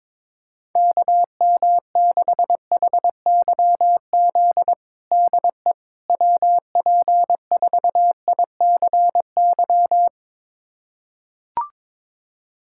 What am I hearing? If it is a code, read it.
Morse code, heard as a keyed tone, about 22 wpm, 700 Hz. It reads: KM6HYZ DE WP4ICY
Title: USA Callsign DE Callsign - 6 Characters - 22wpm